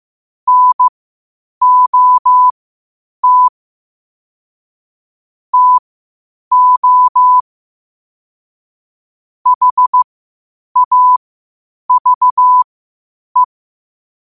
Morse code, heard as a keyed tone, about 15 wpm, spaced as for 8 wpm, 1000 Hz. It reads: NOT TO HAVE
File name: 8wpm